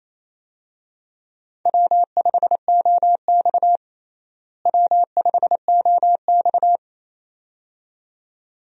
Morse code, heard as a keyed tone, about 28 wpm, 700 Hz. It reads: W5OX W5OX
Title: USA Call Signs - 4 Characters - Repeated 2x - 28wpm